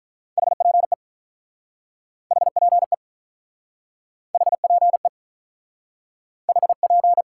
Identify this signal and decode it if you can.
Morse code, heard as a keyed tone, about 48 wpm, 700 Hz. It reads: HPE HPE HPE HP